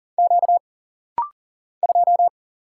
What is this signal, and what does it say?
Morse code, heard as a keyed tone, about 40 wpm, 700 Hz. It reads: Q 2
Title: Single Letter-Number Alt - Rapid-Fire 40wpm